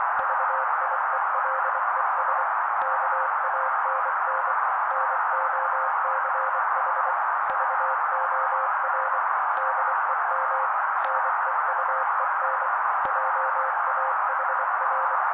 Morse code, heard as a keyed tone, about 24 wpm, 530 Hz. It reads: VIELES KANN NOCH VOR DEM NEUEN JAHR
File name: daily / 23wpm noise